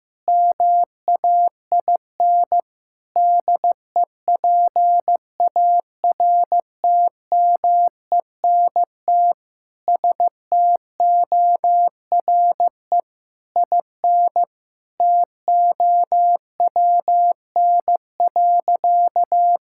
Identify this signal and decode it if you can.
Morse code, heard as a keyed tone, about 15 wpm, 700 Hz. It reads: MAIN DEPARTMENT STORE IN TOWN.
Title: Sentences from Top 1000 Words - Review Part 5 - 15wpm